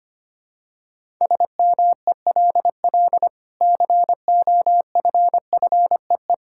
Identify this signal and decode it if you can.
Morse code, heard as a keyed tone, about 25 wpm, 700 Hz. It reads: SMELL COFFEE
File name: Easy Sentences with Hints - 25wpm